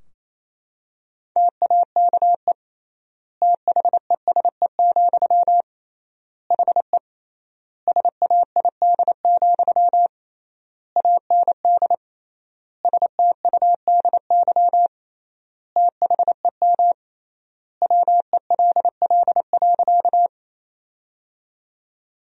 Morse code, heard as a keyed tone, about 28 wpm, 700 Hz. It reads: TAKE THESE, HE SAID, AND STUDY THEM WELL.
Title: 3x Word-Spacing - Sentences from Top 300 Words - 28wpm